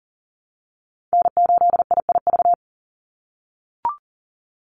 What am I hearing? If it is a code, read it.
Morse code, heard as a keyed tone, about 40 wpm, 700 Hz. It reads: N8IIV
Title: USA Call Signs 40wpm